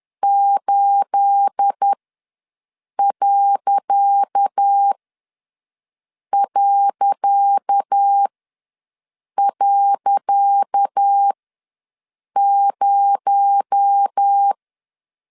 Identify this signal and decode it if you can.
Morse code, heard as a keyed tone, about 10 wpm, spaced as for 6 wpm, 800 Hz. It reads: 8...0
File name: LEZIONI / 10 SIGNS / 10C DOT ws 10wpm 4213